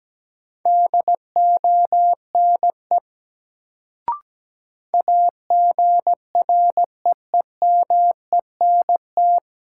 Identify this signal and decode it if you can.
Morse code, heard as a keyed tone, about 17 wpm, 700 Hz. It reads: DONE AGREEMENT
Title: Sets of 5 Words Encore - New Word Drill - 17wpm